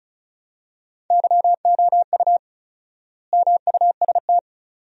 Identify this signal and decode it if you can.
Morse code, heard as a keyed tone, about 36 wpm, 700 Hz. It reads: YOU MUST
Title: 4x Word-Spacing - Sentences from Top 300 Words - 35wpm